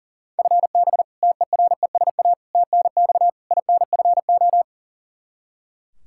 Morse code, heard as a keyed tone, about 40 wpm, 700 Hz. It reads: FB TERESA TNX INFO
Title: Practice QSO - Exchange 3 - More Info - 40wpm